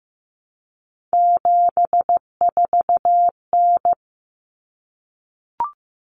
Morse code, heard as a keyed tone, about 15 wpm, 700 Hz. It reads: 74N
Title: Common Cut Numbers - 3-Digit 15wpm